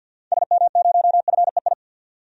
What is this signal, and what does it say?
Morse code, heard as a keyed tone, about 53 wpm, 700 Hz. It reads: SM0FEI
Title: DXCC-100 Call Signs - 6 Characters - 50wpm